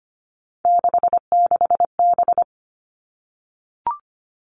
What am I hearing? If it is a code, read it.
Morse code, heard as a keyed tone, about 25 wpm, 700 Hz. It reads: 66B
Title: All Cut Numbers - 3-Digit 25wpm